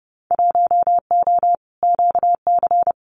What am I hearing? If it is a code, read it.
Morse code, heard as a keyed tone, about 28 wpm, 700 Hz. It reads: 1O QC
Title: Winter Field Day Exchanges 30wpm